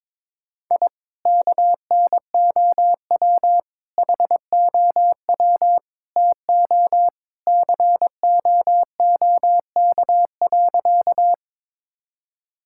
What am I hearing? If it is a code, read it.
Morse code, heard as a keyed tone, about 23 wpm, 700 Hz. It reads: I KNOW HOW TO COOK.
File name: practice / 22wpm / Sentences with 3-Word Phrases - 22wpm